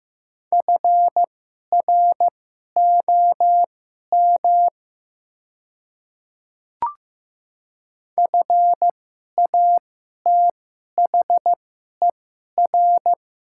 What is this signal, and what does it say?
Morse code, heard as a keyed tone, about 15 wpm, spaced as for 11 wpm, 700 Hz. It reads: FROM FATHER
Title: 2x Character Spacing - Top 200 Words - 15wpm